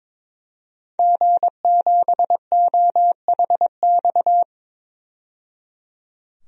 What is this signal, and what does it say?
Morse code, heard as a keyed tone, about 21 wpm, 700 Hz. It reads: G7OHX
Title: DXCC-100 Call Signs - 5 Characters - 22wpm